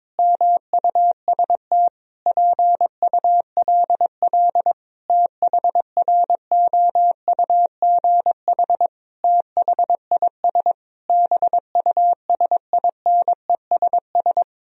Morse code, heard as a keyed tone, about 22 wpm, 700 Hz. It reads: MUST PULL THROUGH THIS BUSINESS
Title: Sentences from Top 1000 Words - Review Part 2 - 22wpm